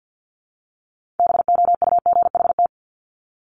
Morse code, heard as a keyed tone, about 50 wpm, 700 Hz. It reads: BOUGHT
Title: Multi-letter Phonemes Repeated 3x 50wpm